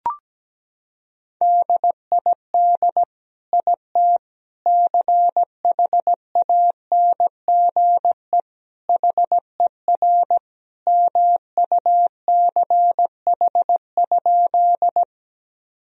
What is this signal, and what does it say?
Morse code, heard as a keyed tone, about 17 wpm, 700 Hz. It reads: DID IT CHANGE HER MUCH?
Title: Sentences from Top 1000 Words - Review Part 1 - 17wpm